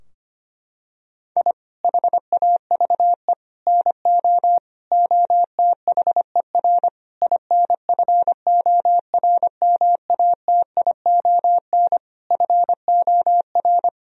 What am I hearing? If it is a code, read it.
Morse code, heard as a keyed tone, about 25 wpm, 700 Hz. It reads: I HAVE NO OTHER INFORMATION FOR
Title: Sentences from Top 1000 Words - Review Part 3 - 25wpm